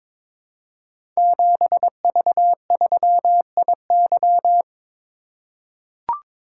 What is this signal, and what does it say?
Morse code, heard as a keyed tone, about 23 wpm, 700 Hz. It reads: 7V3IY
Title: Code Groups - 5 Letter:Numbers - 22wpm